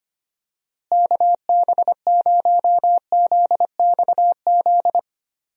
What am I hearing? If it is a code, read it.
Morse code, heard as a keyed tone, about 25 wpm, 700 Hz. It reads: KB0ZXZ
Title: USA Call Signs - 6 Characters - Repeated 3x - 25wpm